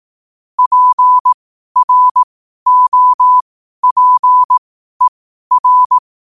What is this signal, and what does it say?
Morse code, heard as a keyed tone, about 18 wpm, spaced as for 12 wpm, 1000 Hz. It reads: PROPER
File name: qotdfiles / 12wpm